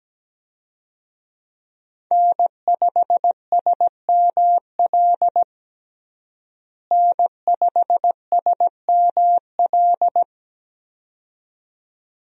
Morse code, heard as a keyed tone, about 17 wpm, 700 Hz. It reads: N5SML N5SML
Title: USA Call Signs - 5 Characters - Repeated 2x - 17wpm